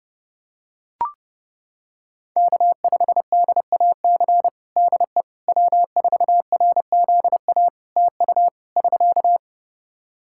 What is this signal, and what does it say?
Morse code, heard as a keyed tone, about 30 wpm, 700 Hz. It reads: K5DAC DE W4RZA TU <SK>
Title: Practice QSO - Exchange 4 - Ending - 30wpm